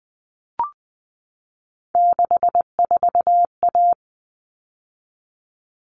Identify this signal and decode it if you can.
Morse code, heard as a keyed tone, about 20 wpm, 700 Hz. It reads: 64A
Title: All Cut Numbers - 3-Digit 20wpm